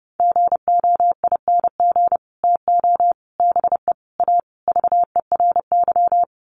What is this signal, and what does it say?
Morse code, heard as a keyed tone, about 30 wpm, 700 Hz. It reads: GOING TO BE A VERY
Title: Sets of 5 Words Encore 30wpm